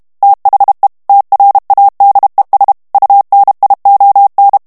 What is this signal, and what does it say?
Morse code, heard as a keyed tone, about 32 wpm, 800 Hz. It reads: THE TRADES UNION
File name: morse mp3 / ANGLAIS texte clair 1 vitesse 30 mots minute